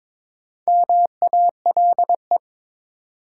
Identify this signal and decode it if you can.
Morse code, heard as a keyed tone, about 22 wpm, 700 Hz. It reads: MALE
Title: Four Letter Words - 22wpm